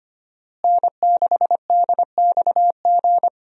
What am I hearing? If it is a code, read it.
Morse code, heard as a keyed tone, about 25 wpm, 700 Hz. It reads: N6DXG
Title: USA Call Signs - 5 Characters - Repeated 3x - 25wpm